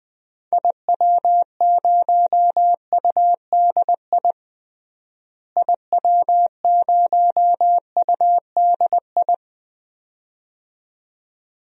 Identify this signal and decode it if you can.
Morse code, heard as a keyed tone, about 20 wpm, 700 Hz. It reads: IW0UDI IW0UDI
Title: DXCC-100 Call Signs - 6 Characters - Repeated 2x - 20wpm